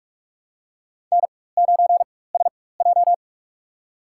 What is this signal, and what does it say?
Morse code, heard as a keyed tone, about 46 wpm, spaced as for 21 wpm, 700 Hz. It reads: N9SJ
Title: USA Call Signs - 4 Characters - 4x Character Spacing - 45wpm